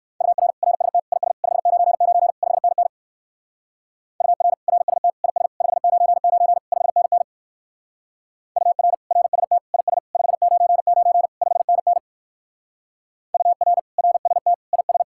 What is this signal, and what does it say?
Morse code, heard as a keyed tone, about 58 wpm, 700 Hz. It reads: UR RST IS 599 5NN UR RST IS 599 5NN UR RST IS 599 5NN UR RST IS